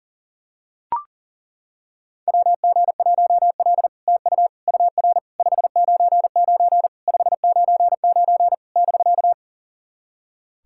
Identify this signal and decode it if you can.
Morse code, heard as a keyed tone, about 40 wpm, 700 Hz. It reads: WG1L TU UR 599 599 <BK>
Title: SOTA QSO Practice 40wpm